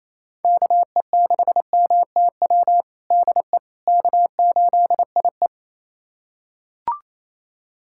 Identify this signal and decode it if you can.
Morse code, heard as a keyed tone, about 28 wpm, 700 Hz. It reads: KE6MTW DE K8IE
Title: USA Callsign DE Callsign 28wpm